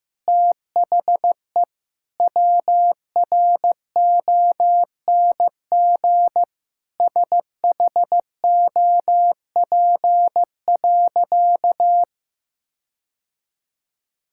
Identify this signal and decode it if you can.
Morse code, heard as a keyed tone, about 16 wpm, 700 Hz. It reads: THE WRONG SHOP.
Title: Sentences from Top 1600 Words 15wpm